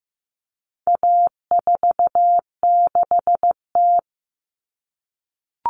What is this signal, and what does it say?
Morse code, heard as a keyed tone, about 15 wpm, 700 Hz. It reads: A46T
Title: Common Cut Numbers - 4-Digit 15wpm